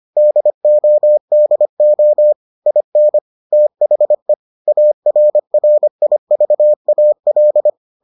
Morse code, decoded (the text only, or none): DODO IN THE ARRIVAL